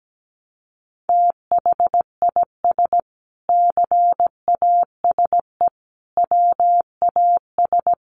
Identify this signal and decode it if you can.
Morse code, heard as a keyed tone, about 17 wpm, 700 Hz. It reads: THIS CASE WAS
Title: Sentences from Top 1100 Words 17wpm